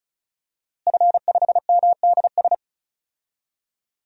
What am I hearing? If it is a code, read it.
Morse code, heard as a keyed tone, about 34 wpm, 700 Hz. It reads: F5MDS
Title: DXCC-100 Call Signs - 5 Characters - Repeated 3x - 35wpm